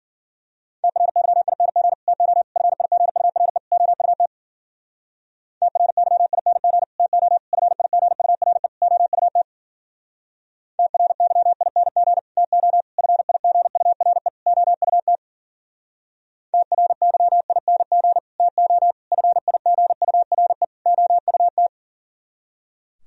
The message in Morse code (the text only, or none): TRYING TO FIGURE OUT TRYING TO FIGURE OUT TRYING TO FIGURE OUT TRYING TO FIGURE OUT